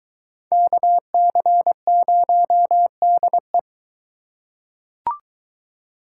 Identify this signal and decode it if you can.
Morse code, heard as a keyed tone, about 22 wpm, 700 Hz. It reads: KC0DE